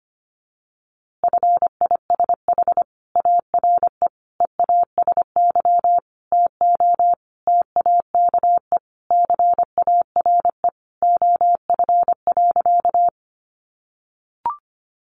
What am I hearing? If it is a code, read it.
Morse code, heard as a keyed tone, about 25 wpm, 700 Hz. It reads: FISH ARE EASY TO TAKE CARE OF.
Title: Easy Phrases 25wpm